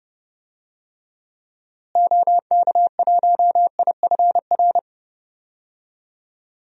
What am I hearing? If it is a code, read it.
Morse code, heard as a keyed tone, about 30 wpm, 700 Hz. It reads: OK1IFR
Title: DXCC-100 Call Signs - 6 Characters - Repeated 3x - 30wpm